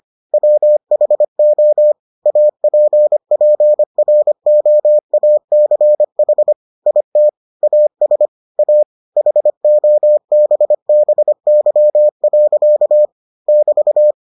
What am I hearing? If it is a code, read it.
Morse code, heard as a keyed tone, about 25 wpm, 600 Hz. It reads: WHO APPROACH IT AS A HOBBY. <BT>